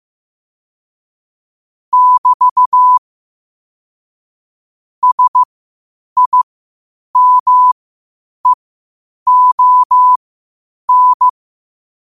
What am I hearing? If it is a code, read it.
Morse code, heard as a keyed tone, about 15 wpm, spaced as for 8 wpm, 1000 Hz. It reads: <BT> SIMEON
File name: qotdfiles / 8wpm